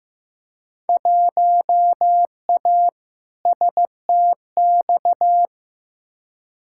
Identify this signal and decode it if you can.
Morse code, heard as a keyed tone, about 15 wpm, 700 Hz. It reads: 1A STX
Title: Field Day Exchanges Encore - 15wpm